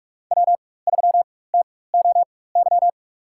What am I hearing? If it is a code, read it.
Morse code, heard as a keyed tone, about 46 wpm, spaced as for 21 wpm, 700 Hz. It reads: W3TOY
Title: USA Call Signs - 5 Characters - 4x Character Spacing - 45wpm